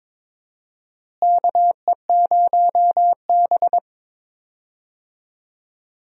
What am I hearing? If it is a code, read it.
Morse code, heard as a keyed tone, about 22 wpm, 700 Hz. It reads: KE0B
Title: USA Call Signs - 4 Characters - Repeated 3x - 22wpm